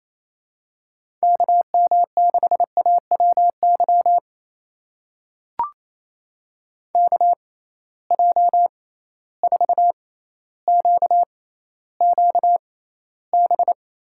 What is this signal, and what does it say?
Morse code, heard as a keyed tone, about 28 wpm, spaced as for 14 wpm, 700 Hz. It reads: KM6AWY KJ4QQB
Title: USA Call Signs - 6 Characters - 6x Character Spacing - 28wpm